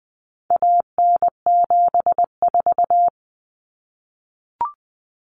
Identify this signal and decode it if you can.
Morse code, heard as a keyed tone, about 20 wpm, 700 Hz. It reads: AN74
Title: Common Cut Numbers - 4-Digit 20wpm